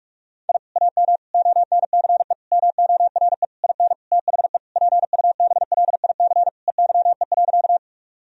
Morse code, heard as a keyed tone, about 42 wpm, 700 Hz. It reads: I AM ONCE MORE IN THE PUBLIC EYE.
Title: Sentences from Top 1000 Words - Review Part 3 - 45wpm